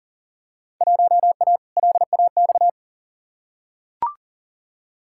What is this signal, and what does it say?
Morse code, heard as a keyed tone, about 40 wpm, 700 Hz. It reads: 1A LAX
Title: Field Day Exchanges - 40wpm